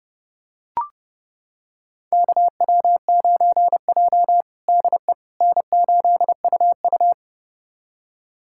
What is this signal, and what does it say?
Morse code, heard as a keyed tone, about 32 wpm, 700 Hz. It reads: KW9J DE N8UU
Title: USA Callsign DE Callsign - 4 Characters - 30wpm